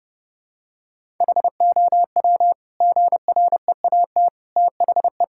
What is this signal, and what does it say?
Morse code, heard as a keyed tone, about 32 wpm, 700 Hz. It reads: HOW GREAT THE
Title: Sentences from Top 1000 Words - Review Part 5 - 30wpm